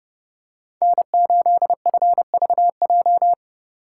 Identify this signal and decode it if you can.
Morse code, heard as a keyed tone, about 29 wpm, 700 Hz. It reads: N8FVJ